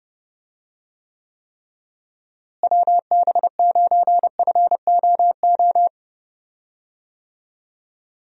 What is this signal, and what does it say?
Morse code, heard as a keyed tone, about 31 wpm, 700 Hz. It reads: WB9FOO
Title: USA Call Signs - 6 Characters - Repeated 3x - 30wpm